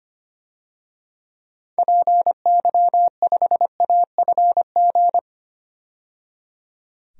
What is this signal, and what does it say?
Morse code, heard as a keyed tone, about 25 wpm, 690 Hz. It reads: PY5AFG
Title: DXCC-100 Call Signs - 6 Characters - 25wpm